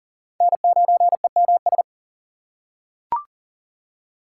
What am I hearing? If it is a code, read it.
Morse code, heard as a keyed tone, about 40 wpm, 700 Hz. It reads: N9EMS
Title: USA Call Signs - 5 Characters - 2x Character Spacing - 40wpm